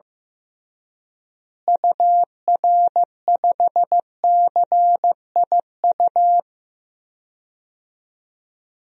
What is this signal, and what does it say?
Morse code, heard as a keyed tone, about 15 wpm, 700 Hz. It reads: UR5CIU
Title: DXCC-100 Call Signs - 6 Characters - Repeated 3x - 15wpm